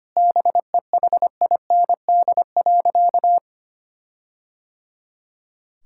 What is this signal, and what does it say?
Morse code, heard as a keyed tone, about 25 wpm, 700 Hz. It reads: BEHIND.
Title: Sentences from Top 1600 Words 25wpm